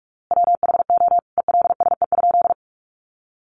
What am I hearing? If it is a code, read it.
Morse code, heard as a keyed tone, about 44 wpm, 700 Hz. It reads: WHO ELSE?
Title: A Sound Of Thunder 45wpm